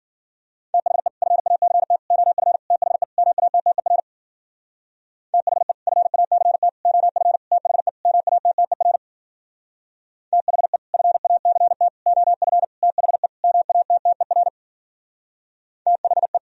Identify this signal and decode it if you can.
Morse code, heard as a keyed tone, about 52 wpm, 700 Hz. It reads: THE FACT OF THE MATTER THE FACT OF THE MATTER THE FACT OF THE MATTER THE